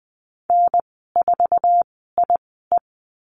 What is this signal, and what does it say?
Morse code, heard as a keyed tone, about 20 wpm, spaced as for 15 wpm, 700 Hz. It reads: N4IE
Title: 055 Lesson 10g - Letter r - Call Signs 20wpm